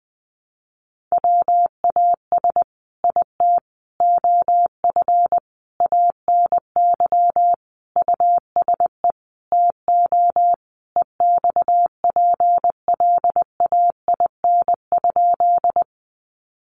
Morse code, heard as a keyed tone, about 20 wpm, 700 Hz. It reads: WAS IT OF ANY USE TO EXPLAIN?
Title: Sentences from Top 1100 Words 20wpm